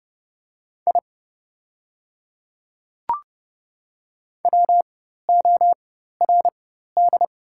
Morse code, heard as a keyed tone, about 30 wpm, spaced as for 14 wpm, 700 Hz. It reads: I WORD
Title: 4x Character Spacing - Top 100 Words - 30wpm